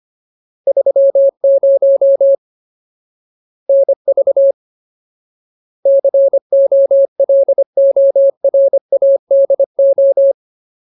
Morse code, heard as a keyed tone, about 25 wpm, 550 Hz. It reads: 30 NV COLORADO
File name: Ultimate Word List / 25WPM Ultimate